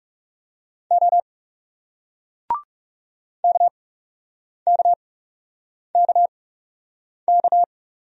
Morse code, heard as a keyed tone, about 38 wpm, spaced as for 8 wpm, 700 Hz. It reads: O KKKK